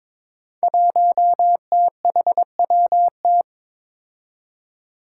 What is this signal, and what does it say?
Morse code, heard as a keyed tone, about 22 wpm, 700 Hz. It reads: 1THWT
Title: Five Challenging Letter-Number Combinations 22wpm